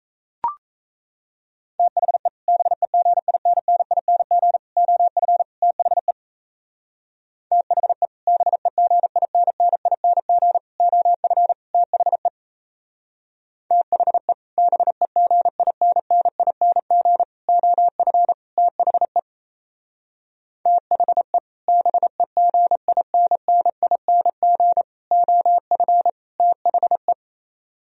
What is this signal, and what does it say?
Morse code, heard as a keyed tone, about 36 wpm, 700 Hz. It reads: THE BEGINNING OF THE THE BEGINNING OF THE THE BEGINNING OF THE THE BEGINNING OF THE